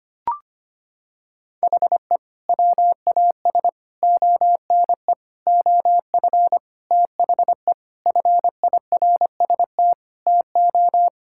Morse code, heard as a keyed tone, about 25 wpm, 700 Hz. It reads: HE WAS ONE OF THE FIRST TO
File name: practice / 25wpm / Sentences with 3-Word Phrases - 25wpm